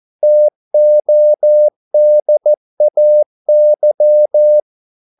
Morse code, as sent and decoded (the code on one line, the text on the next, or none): - --- -.. .- -.--
TODAY